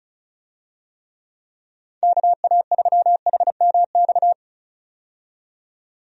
Morse code, heard as a keyed tone, about 34 wpm, 700 Hz. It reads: KA3HMX
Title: USA Call Signs - 6 Characters - 35wpm